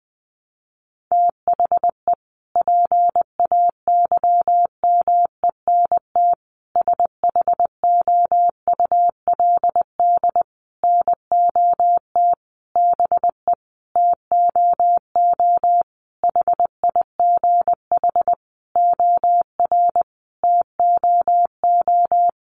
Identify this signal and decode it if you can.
Morse code, heard as a keyed tone, about 20 wpm, 700 Hz. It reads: THE PAYMENT SHOULD NOT BE TOO HIGH OR TOO
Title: Sentences from Top 1200 Words 20wpm